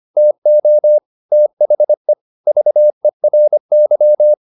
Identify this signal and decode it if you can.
Morse code, heard as a keyed tone, about 25 wpm, 600 Hz. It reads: TO THE VERY
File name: the old man and the sea part 1 25wpm